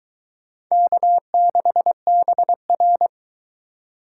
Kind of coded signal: Morse code

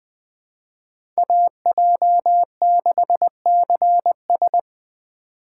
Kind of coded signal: Morse code

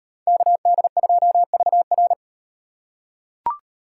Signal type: Morse code

